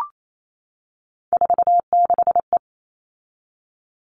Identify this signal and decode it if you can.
Morse code, heard as a keyed tone, about 28 wpm, 700 Hz. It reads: 46E
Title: All Cut Numbers - 3-Digit 28wpm